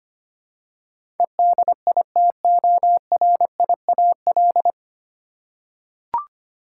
Morse code, heard as a keyed tone, about 25 wpm, 700 Hz. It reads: EDITORIAL
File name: practice / 25wpm / 9 Letter Words - First 3 Letters Given - 25wpm